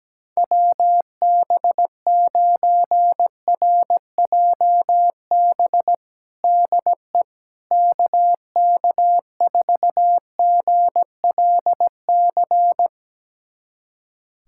Morse code, heard as a keyed tone, about 17 wpm, 700 Hz. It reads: WB9RJB DE KK4GLC
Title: USA Callsign DE Callsign 17wpm